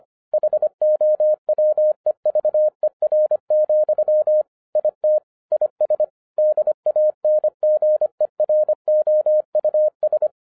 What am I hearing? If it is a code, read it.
Morse code, heard as a keyed tone, about 25 wpm, 610 Hz. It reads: HOWEVER, IT IS DANGEROUS
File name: 2011-10-mp3 / n09-CW TO PS2 CONVERTER 25wpm